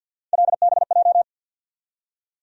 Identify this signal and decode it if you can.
Morse code, heard as a keyed tone, about 52 wpm, 700 Hz. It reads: LBJ
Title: USA Call Sign Postfixes - 3 Letters 50wpm